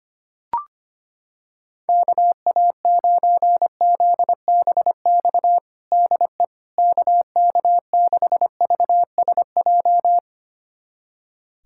Morse code, heard as a keyed tone, about 25 wpm, 700 Hz. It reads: KA9ZBX DE KK6VSJ
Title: USA Callsign DE Callsign - 6 Characters - 25wpm